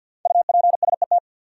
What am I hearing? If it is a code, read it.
Morse code, heard as a keyed tone, about 52 wpm, 700 Hz. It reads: UPSET
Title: Sentences from Top 800 Words - New Word Drill - 50wpm